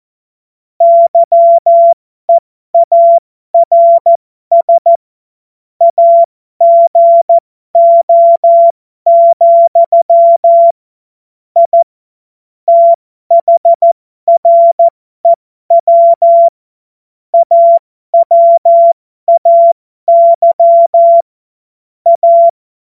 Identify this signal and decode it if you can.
Morse code, heard as a keyed tone, about 14 wpm, spaced as for 12 wpm, 680 Hz. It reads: YEARS AGO, I THREW AWAY A
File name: Morsekurs / Witze